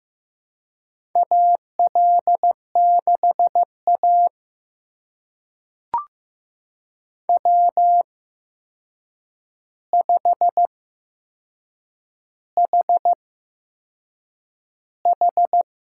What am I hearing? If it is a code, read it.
Morse code, heard as a keyed tone, about 15 wpm, spaced as for 6 wpm, 700 Hz. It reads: AL6A W5HH
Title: USA Call Signs - 4 Characters - 8x Character Spacing - 15wpm